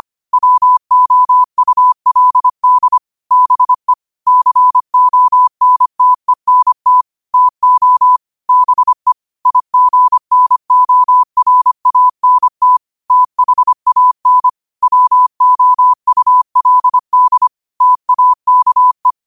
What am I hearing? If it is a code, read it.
Morse code, heard as a keyed tone, about 25 wpm, 1000 Hz. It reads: WOULD BE CONTENT TO BE IGNORANT THAN WOULD TAKE
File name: qotdfiles / 25wpm